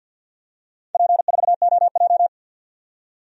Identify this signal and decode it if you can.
Morse code, heard as a keyed tone, about 52 wpm, 700 Hz. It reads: P4OJ
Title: DXCC-100 Call Signs - 4 Characters - 50wpm